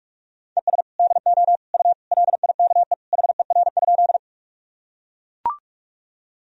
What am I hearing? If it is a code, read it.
Morse code, heard as a keyed tone, about 41 wpm, 700 Hz. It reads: ES DO U LIKE HER?
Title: Sentences from Top 100 Words and Abbreviations 45wpm